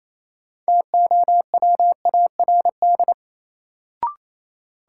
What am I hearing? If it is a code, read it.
Morse code, heard as a keyed tone, about 28 wpm, 700 Hz. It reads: TOWARD
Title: Sentences from Top 1000 Words - Review Part 6 - New Word Drill - 28wpm